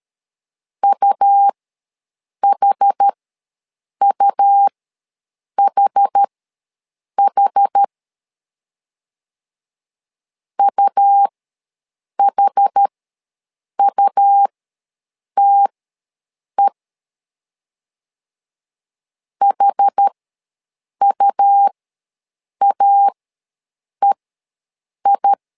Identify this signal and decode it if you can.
Morse code, heard as a keyed tone, about 13 wpm, spaced as for 7 wpm, 800 Hz. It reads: UHUHH UHUTE HUAEI
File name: LEZIONI / 2 UHDN / 2 UH ws 12wpm 1730